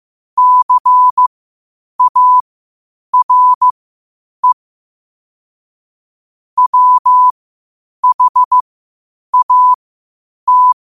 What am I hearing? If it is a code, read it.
Morse code, heard as a keyed tone, about 15 wpm, spaced as for 8 wpm, 1000 Hz. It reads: CARE WHAT
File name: qotdfiles / 8wpm